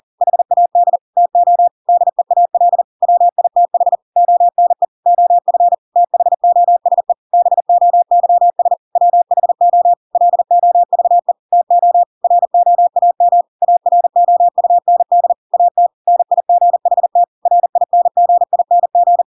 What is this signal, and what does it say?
Morse code, heard as a keyed tone, about 40 wpm, 700 Hz. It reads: HAD TO DEAL WITH ONE OF THOSE BOYS WHO LOVE TO ROAM AROUND AT NIGHT RINGING
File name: Pinocchio / Pinocchio ch6 40 40 ik1hge